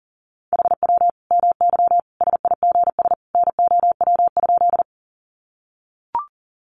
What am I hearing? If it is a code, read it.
Morse code, heard as a keyed tone, about 40 wpm, 700 Hz. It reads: HW MY SIGS NOW?
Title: Common QSO Phrases 40wpm